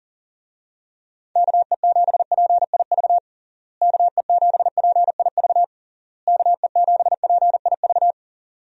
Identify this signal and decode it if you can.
Morse code, heard as a keyed tone, about 40 wpm, 700 Hz. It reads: KE7PIV KE7PIV KE7PIV
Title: USA Call Signs - 6 Characters - Repeated 3x - 40wpm